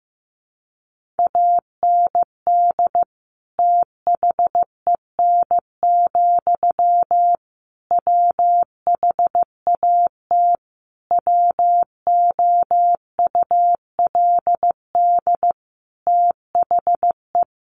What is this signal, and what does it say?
Morse code, heard as a keyed tone, about 15 wpm, 700 Hz. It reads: AND THEN, WHAT WOULD THE
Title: Sentences from Top 1500 Words 15wpm